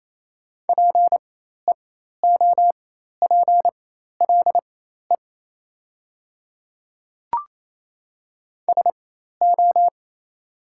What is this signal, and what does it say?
Morse code, heard as a keyed tone, about 28 wpm, spaced as for 13 wpm, 700 Hz. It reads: PEOPLE SO
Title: 4x Character Spacing - Top 100 Words - 28wpm